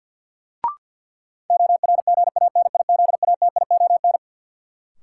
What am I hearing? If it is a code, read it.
Morse code, heard as a keyed tone, about 52 wpm, 700 Hz. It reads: ORGANIZATION
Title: Sentences from Top 1100 Words - New Word Drill Alt - 50wpm